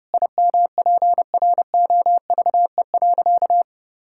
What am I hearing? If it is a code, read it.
Morse code, heard as a keyed tone, about 32 wpm, 700 Hz. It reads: IMPROVE.
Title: Sentences from Top 1000 Words - Review Part 5 - 30wpm